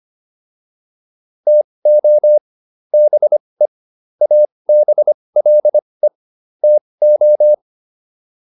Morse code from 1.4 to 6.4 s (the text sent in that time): TO BE ABLE